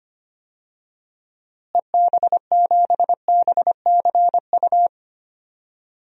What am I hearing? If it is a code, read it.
Morse code, heard as a keyed tone, about 25 wpm, 700 Hz. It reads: EB7BCU
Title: DXCC-100 Call Signs - 6 Characters - Repeated 3x - 25wpm